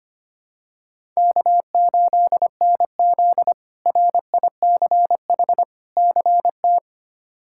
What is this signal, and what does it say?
Morse code, heard as a keyed tone, about 25 wpm, 700 Hz. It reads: K8NZ RICH CT
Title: K1USN SST QSO Practice - 25wpm